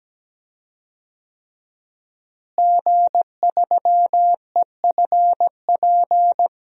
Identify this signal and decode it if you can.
Morse code, heard as a keyed tone, about 17 wpm, 700 Hz. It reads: G3EFP
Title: DXCC-100 Call Signs - 5 Characters - Repeated 3x - 17wpm